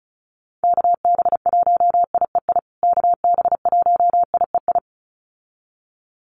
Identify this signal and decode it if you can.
Morse code, heard as a keyed tone, about 34 wpm, 700 Hz. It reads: KB1IEI KB1IEI
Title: USA Call Signs Repeated 2x 35wpm